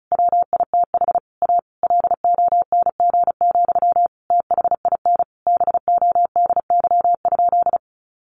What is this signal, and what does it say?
Morse code, heard as a keyed tone, about 34 wpm, 700 Hz. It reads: WITH A LONG, THIN BODY?
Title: Sentences from Top 1200 Words 35wpm